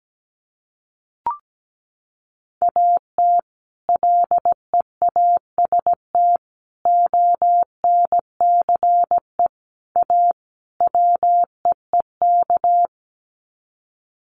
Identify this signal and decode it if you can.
Morse code, heard as a keyed tone, about 17 wpm, 700 Hz. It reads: AT LEAST ONCE A WEEK
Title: Sets of 5 Words Encore2 17wpm